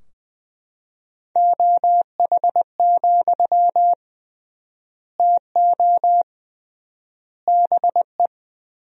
Morse code, heard as a keyed tone, about 20 wpm, 700 Hz. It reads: OH, TO BE
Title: 3x Word-Spacing - Sentences from Top 500 Words - 20wpm